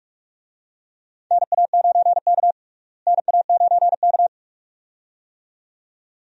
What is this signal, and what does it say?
Morse code, heard as a keyed tone, about 42 wpm, 700 Hz. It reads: NA9K NA9K
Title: USA Call Signs - 4 Characters - Repeated 2x - 45wpm